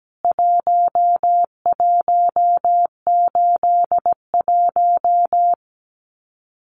Morse code, heard as a keyed tone, about 17 wpm, 700 Hz. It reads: 1181
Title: CWT Exchange 17wpm